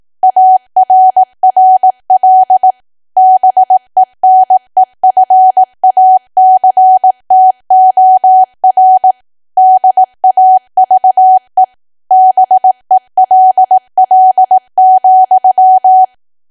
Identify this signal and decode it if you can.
Morse code, heard as a keyed tone, about 18 wpm, 750 Hz. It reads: ARRL BENEFACTOR DAVE BELL,
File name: Archive / 18 WPM / 18WPM